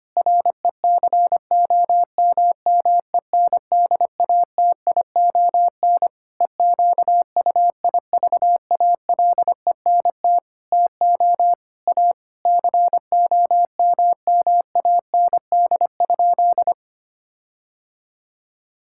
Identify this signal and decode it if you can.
Morse code, heard as a keyed tone, about 25 wpm, 700 Hz. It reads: RECOMMENDATION EQUIVALENT TO A COMMAND?
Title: Sentences from Top 1600 Words 25wpm